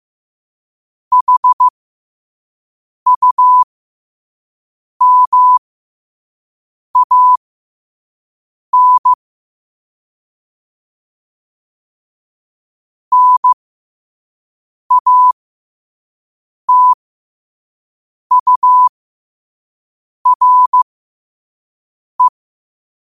Morse code, heard as a keyed tone, about 15 wpm, spaced as for 5 wpm, 1000 Hz. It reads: HUMAN NATURE
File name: qotdfiles / 5wpm